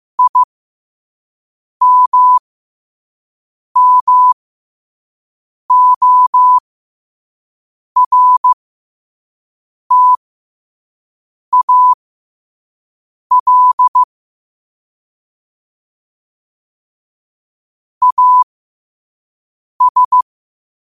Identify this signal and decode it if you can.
Morse code, heard as a keyed tone, about 15 wpm, spaced as for 5 wpm, 1000 Hz. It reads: IMMORTAL AS
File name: qotdfiles / 5wpm